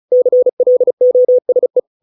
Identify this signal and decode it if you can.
Morse code, heard as a keyed tone, about 35 wpm, 500 Hz. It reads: CLOSE